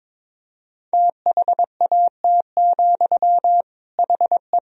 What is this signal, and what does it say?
Morse code, heard as a keyed tone, about 21 wpm, 700 Hz. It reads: THAT, HE
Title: Sentences from Top 1000 Words - Review Part 3 - 22wpm